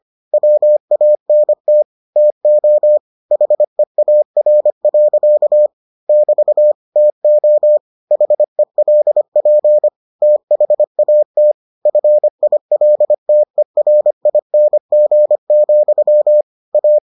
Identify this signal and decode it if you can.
Morse code, heard as a keyed tone, about 25 wpm, 600 Hz. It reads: WANT TO HEAR. <BT> TO HELP THAT FILTERING, A